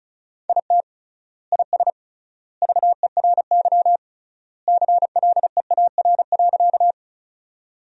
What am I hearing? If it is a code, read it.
Morse code, heard as a keyed tone, about 34 wpm, 700 Hz. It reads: IT IS VERY CLEAR.
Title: 3x Word-Spacing - Sentences from Top 500 Words - 35wpm